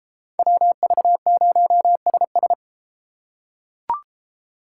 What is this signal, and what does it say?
Morse code, heard as a keyed tone, about 32 wpm, 700 Hz. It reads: WV0SS